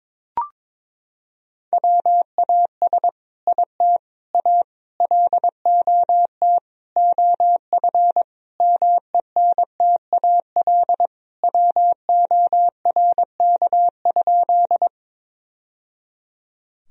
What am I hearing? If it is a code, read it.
Morse code, heard as a keyed tone, about 21 wpm, 700 Hz. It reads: WAS IT A LOT OF MENTAL WORK?
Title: Sentences from Top 1000 Words - Review Part 4 - 22wpm